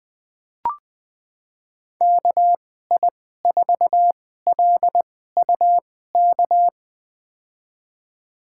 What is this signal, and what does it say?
Morse code, heard as a keyed tone, about 21 wpm, spaced as for 15 wpm, 700 Hz. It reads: KI4LUK
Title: USA Call Signs - 6 Characters - 2x Character Spacing - 20wpm